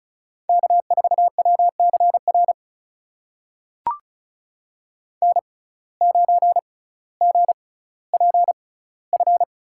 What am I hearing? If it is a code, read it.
Morse code, heard as a keyed tone, about 34 wpm, spaced as for 18 wpm, 700 Hz. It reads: K4WCR N9GPF
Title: USA Call Signs - 5 Characters - 6x Character Spacing - 35wpm